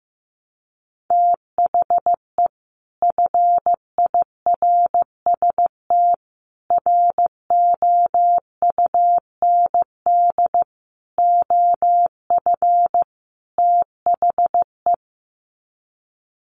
Morse code, heard as a keyed tone, about 15 wpm, 700 Hz. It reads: THE FIRST ROUND OF THE
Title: Sets of 5 Words Encore 15wpm